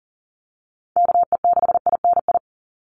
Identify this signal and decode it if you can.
Morse code, heard as a keyed tone, about 40 wpm, 700 Hz. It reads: KE6INI
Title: USA Call Signs 40wpm